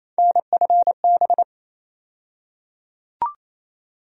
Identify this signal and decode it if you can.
Morse code, heard as a keyed tone, about 28 wpm, 700 Hz. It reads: NFB
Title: USA Call Sign Postfixes - 3 Letters 28wpm